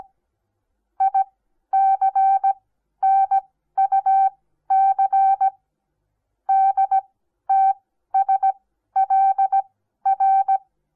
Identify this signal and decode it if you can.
Morse code, heard as a keyed tone, about 17 wpm, spaced as for 12 wpm, 780 Hz. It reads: ICNUC DTSLR
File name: Onlinelehrgang / morsen / DJ4UF Morsen08